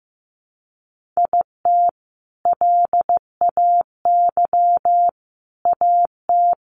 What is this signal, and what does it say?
Morse code, heard as a keyed tone, about 15 wpm, 700 Hz. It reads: IT LAY AT
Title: Sentences from Top 1500 Words 15wpm